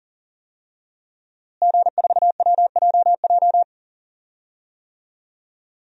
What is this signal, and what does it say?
Morse code, heard as a keyed tone, about 40 wpm, 700 Hz. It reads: G4WJJ
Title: DXCC-100 Call Signs - 5 Characters - Repeated 3x - 40wpm